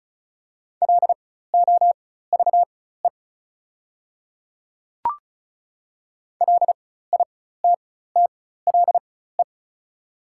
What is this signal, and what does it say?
Morse code, heard as a keyed tone, about 34 wpm, spaced as for 16 wpm, 690 Hz. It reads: LOVE LITTLE
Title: 4x Character Spacing - Top 500 Words - 35wpm